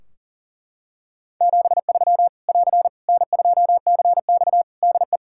Morse code, heard as a keyed tone, about 38 wpm, 700 Hz. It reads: 73 <AR> N2CX DE
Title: Practice QSO - Exchange 4 - Ending - 40wpm